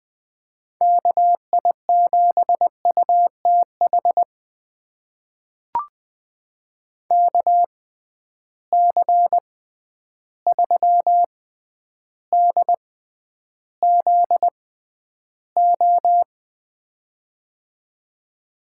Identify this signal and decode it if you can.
Morse code, heard as a keyed tone, about 21 wpm, spaced as for 10 wpm, 700 Hz. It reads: KI7UTH KC3DZO
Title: USA Call Signs - 6 Characters - 6x Character Spacing - 20wpm